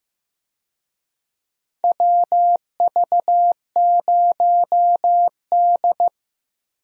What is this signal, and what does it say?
Morse code, heard as a keyed tone, about 15 wpm, 700 Hz. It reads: WV0D
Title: USA Call Signs - 4 Characters - Repeated 3x - 15wpm